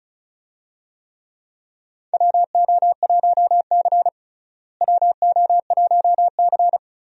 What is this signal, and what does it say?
Morse code, heard as a keyed tone, about 36 wpm, 700 Hz. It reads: WO1C WO1C
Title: USA Call Signs - 4 Characters - Repeated 3x - 35wpm